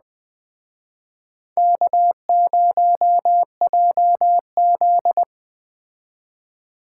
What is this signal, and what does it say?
Morse code, heard as a keyed tone, about 21 wpm, 700 Hz. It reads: K0JZ